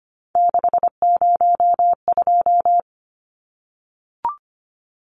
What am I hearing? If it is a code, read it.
Morse code, heard as a keyed tone, about 25 wpm, 700 Hz. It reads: 602
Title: Numbers - 3-Digit 25wpm